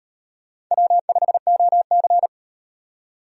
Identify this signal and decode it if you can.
Morse code, heard as a keyed tone, about 36 wpm, 700 Hz. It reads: W5OC